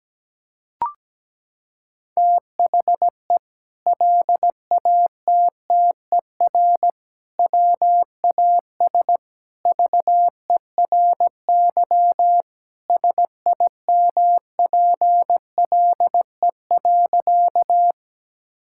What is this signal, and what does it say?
Morse code, heard as a keyed tone, about 17 wpm, 700 Hz. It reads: THE LATTER WAS VERY SIMPLE.
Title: Sentences from Top 1600 Words 17wpm